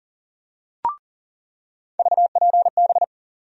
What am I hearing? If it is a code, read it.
Morse code, heard as a keyed tone, about 40 wpm, 700 Hz. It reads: VPB
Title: USA Call Sign Postfixes - 3 Letters 40wpm